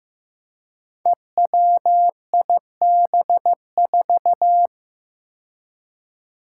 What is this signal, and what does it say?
Morse code, heard as a keyed tone, about 15 wpm, 700 Hz. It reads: EWIB4
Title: Five Challenging Letter-Number Combinations 15wpm